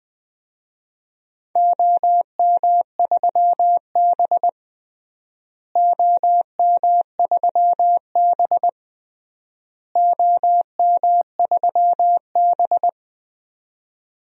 Morse code, heard as a keyed tone, about 21 wpm, 700 Hz. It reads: OM3B OM3B OM3B
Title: DXCC-100 Call Signs - 4 Characters - Repeated 3x - 20wpm